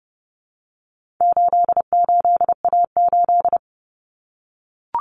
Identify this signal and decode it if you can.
Morse code, heard as a keyed tone, about 30 wpm, 700 Hz. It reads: 88A8
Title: Common Cut Numbers - 4-Digit 30wpm